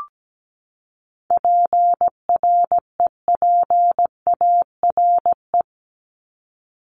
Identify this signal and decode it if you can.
Morse code, heard as a keyed tone, about 17 wpm, 700 Hz. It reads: PREPARE
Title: ED Suffix - Top 100 Words with Base Words 17wpm